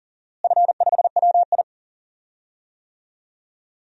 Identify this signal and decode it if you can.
Morse code, heard as a keyed tone, about 37 wpm, 700 Hz. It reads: F5WI
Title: DXCC-100 Call Signs - 4 Characters - Repeated 3x - 40wpm